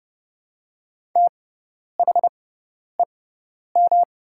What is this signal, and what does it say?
Morse code, heard as a keyed tone, about 31 wpm, spaced as for 10 wpm, 700 Hz. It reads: THEM
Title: 6x Character Spacing - Top 100 Words - 30wpm